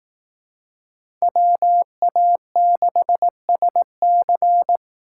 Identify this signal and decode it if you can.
Morse code, heard as a keyed tone, about 19 wpm, 700 Hz. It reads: WA6SC
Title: Speed Racing - USA 2x2 Call Signs 15wpm